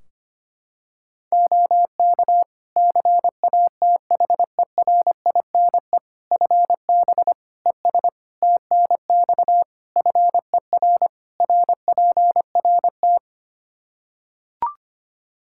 Morse code, heard as a keyed tone, about 25 wpm, 700 Hz. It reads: OK CATHERINE FB ES TNX FER RPRT
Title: Practice QSO - Exchange 2 - Info - 25wpm